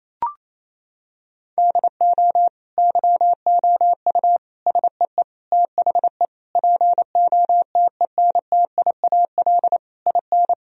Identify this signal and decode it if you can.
Morse code, heard as a keyed tone, about 28 wpm, 700 Hz. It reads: DO YOU SEE THE POTENTIAL IN
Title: Sentences from Top 1000 Words - Review Part 4 - 28wpm